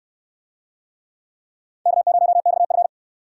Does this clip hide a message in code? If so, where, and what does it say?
Yes: Morse code from 1.9 to 2.9 s, heard as a keyed tone, about 68 wpm, 700 Hz. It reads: K0XF